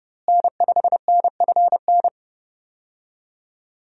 Morse code, heard as a keyed tone, about 30 wpm, 700 Hz. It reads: N5NFN
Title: USA Call Signs - 5 Characters - Repeated 3x - 30wpm